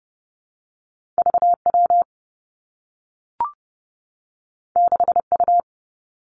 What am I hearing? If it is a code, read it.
Morse code, heard as a keyed tone, about 30 wpm, 700 Hz. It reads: V W 6 U
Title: Two Challenging Letter-Number Combinations 30wpm